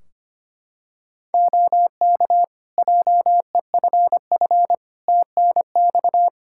Morse code, heard as a keyed tone, about 25 wpm, 700 Hz. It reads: OK JEFF TNX
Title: Practice QSO - Exchange 4 - Ending - 25wpm